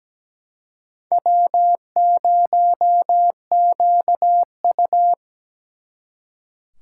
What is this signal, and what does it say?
Morse code, heard as a keyed tone, about 20 wpm, 700 Hz. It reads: W0QU
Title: Speed Racing - USA 1x2 Call Signs 17wpm